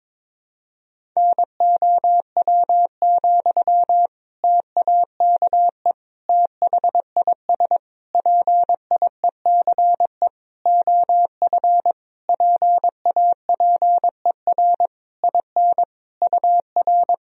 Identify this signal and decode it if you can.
Morse code, heard as a keyed tone, about 22 wpm, 700 Hz. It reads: NOW, TAKE THIS PIECE OF PAPER IN UR
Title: Sentences from Top 400 Words and Abbreviations 22wpm